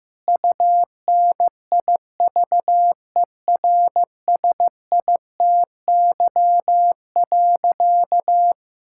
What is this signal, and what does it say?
Morse code, heard as a keyed tone, about 15 wpm, 700 Hz. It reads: UNIVERSITY.
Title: Sentences from Top 1000 Words - Review Part 5 - 15wpm